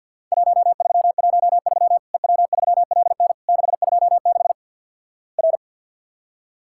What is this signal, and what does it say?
Morse code, heard as a keyed tone, about 49 wpm, 700 Hz. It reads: 1313 EW3LN 626 R
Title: Worked All Europe DX Contest - QTC Exchange with Repeats - 50wpm